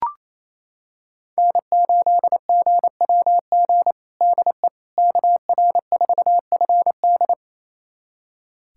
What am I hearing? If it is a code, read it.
Morse code, heard as a keyed tone, about 27 wpm, 700 Hz. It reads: N8GWG DE KR4FD
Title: USA Callsign DE Callsign - 5 Characters - 28wpm